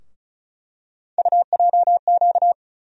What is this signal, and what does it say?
Morse code, heard as a keyed tone, about 36 wpm, 700 Hz. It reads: UJQ
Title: USA Call Sign Postfixes - 3 Letters 35wpm